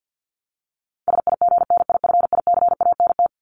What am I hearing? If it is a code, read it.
Morse code, heard as a keyed tone, about 50 wpm, 700 Hz. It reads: SIGNIFICANT
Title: LY Suffix - Top 100 Words with Base Words 50wpm